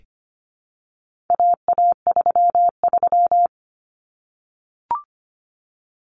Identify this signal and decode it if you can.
Morse code, heard as a keyed tone, about 25 wpm, 700 Hz. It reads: AA33
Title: Common Cut Numbers - 4-Digit 25wpm